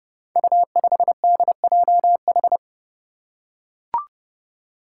Morse code, heard as a keyed tone, about 28 wpm, 700 Hz. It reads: U5DJH
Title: Five Challenging Letter-Number Combinations 30wpm